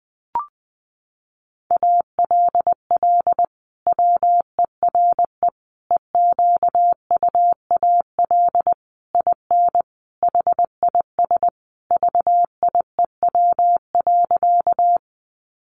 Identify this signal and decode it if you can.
Morse code, heard as a keyed tone, about 20 wpm, 700 Hz. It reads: ALL WERE EQUAL IN HIS VIEW.
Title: Sentences from Top 1400 Words 20wpm